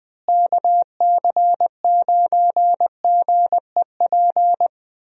Morse code, heard as a keyed tone, about 20 wpm, 700 Hz. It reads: KC9GEP
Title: USA Call Signs - 6 Characters - Repeated 2x - 20wpm